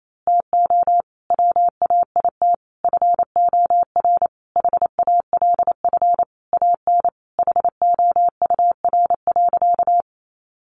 Morse code, heard as a keyed tone, about 28 wpm, 700 Hz. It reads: TO WAIT FOR HALF AN HOUR.
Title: Sentences from Top 1100 Words 28wpm